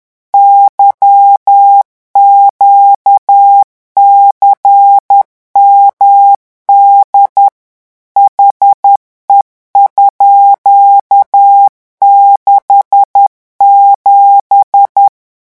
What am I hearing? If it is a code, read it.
Morse code, heard as a keyed tone, about 10 wpm, 800 Hz. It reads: YQCMD HE_67
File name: morse mp3 / revision13-vitesse 10 mots